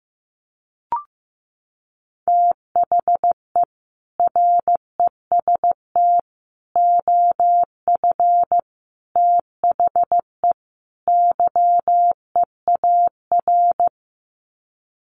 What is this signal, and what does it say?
Morse code, heard as a keyed tone, about 15 wpm, 700 Hz. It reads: THE REST OF THE YEAR
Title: Sets of 5 Words Encore 15wpm